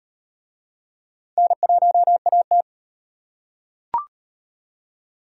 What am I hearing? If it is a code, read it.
Morse code, heard as a keyed tone, about 38 wpm, 700 Hz. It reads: N1AT